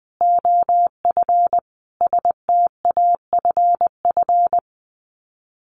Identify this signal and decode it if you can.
Morse code, heard as a keyed tone, about 20 wpm, 700 Hz. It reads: OF STAFF
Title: Sets of 5 Words Encore 20wpm